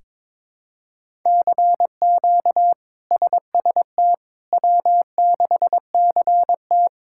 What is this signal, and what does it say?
Morse code, heard as a keyed tone, about 22 wpm, 700 Hz. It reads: CQ SST W6CT
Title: K1USN SST QSO Practice - 22wpm